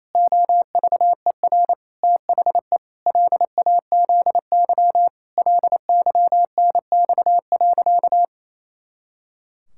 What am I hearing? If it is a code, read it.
Morse code, heard as a keyed tone, about 28 wpm, 700 Hz. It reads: OVER THE LAZY LYNX.
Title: Pangram Challenge v2 - 28wpm